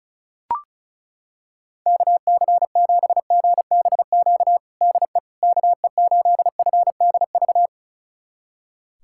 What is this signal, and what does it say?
Morse code, heard as a keyed tone, about 34 wpm, 700 Hz. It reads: KC7GBQ DE KE8FDV
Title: USA Callsign DE Callsign - 6 Characters - 35wpm